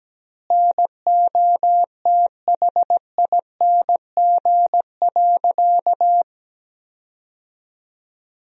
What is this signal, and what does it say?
Morse code, heard as a keyed tone, about 17 wpm, 700 Hz. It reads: NOTHING.
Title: 3x Word-Spacing - Sentences from Top 500 Words - 17wpm